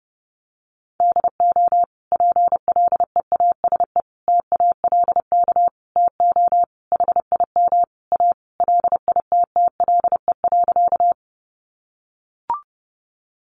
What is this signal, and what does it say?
Morse code, heard as a keyed tone, about 30 wpm, 700 Hz. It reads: DO PLEASE TALK TO HIM A LITTLE.
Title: Sentences from Top 1500 Words 30wpm